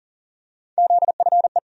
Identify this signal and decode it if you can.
Morse code, heard as a keyed tone, about 40 wpm, 700 Hz. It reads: ZFE
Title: USA Call Sign Postfixes - 3 Letters 40wpm